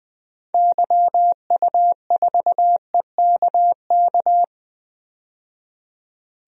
Morse code, heard as a keyed tone, about 20 wpm, 700 Hz. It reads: YU4EKK
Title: DXCC-100 Call Signs - 6 Characters - 20wpm